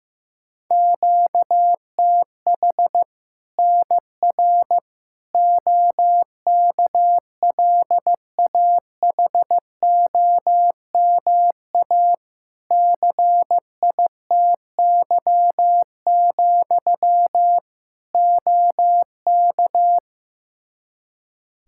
Morse code, heard as a keyed tone, about 15 wpm, 700 Hz. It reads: QTH NR OKLAHOMA CITY, OK
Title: Practice QSO - Exchange 1 - Beginning - 15wpm